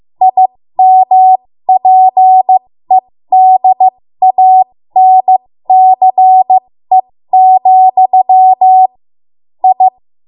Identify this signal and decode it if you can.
Morse code, heard as a keyed tone, about 15 wpm, spaced as for 13 wpm, 750 Hz. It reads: IMPEDANCE, I
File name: Morse / 13WPM